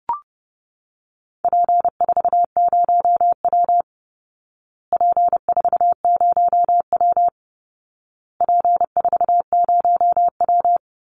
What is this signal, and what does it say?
Morse code, heard as a keyed tone, about 30 wpm, 700 Hz. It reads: P40W P40W P40W
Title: Call Signs Repeated 2x - 30wpm